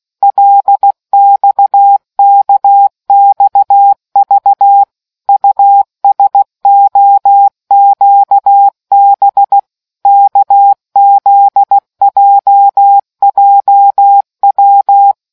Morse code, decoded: LXKXV USOQB KZJJW